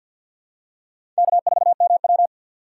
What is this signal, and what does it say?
Morse code, heard as a keyed tone, about 53 wpm, 700 Hz. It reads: K4MW
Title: USA Call Signs - 2x Character Spacing - 50wpm